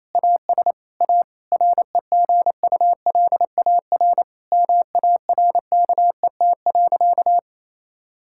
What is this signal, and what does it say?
Morse code, heard as a keyed tone, about 28 wpm, 700 Hz. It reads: AS A REGULAR MARKET.
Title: Sentences from Top 1000 Words - Review Part 6 - 28wpm